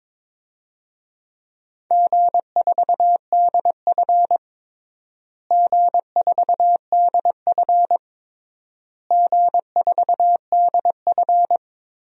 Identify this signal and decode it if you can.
Morse code, heard as a keyed tone, about 22 wpm, 700 Hz. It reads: G4DF G4DF G4DF
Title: DXCC-100 Call Signs - 4 Characters - Repeated 3x - 22wpm